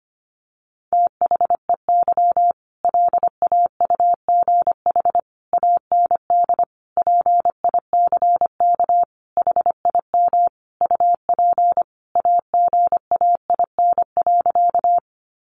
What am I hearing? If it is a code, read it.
Morse code, heard as a keyed tone, about 25 wpm, 700 Hz. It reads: THEY LAUGH AND PICK HIM UP AGAIN.
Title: Sentences from Top 1000 Words 25wpm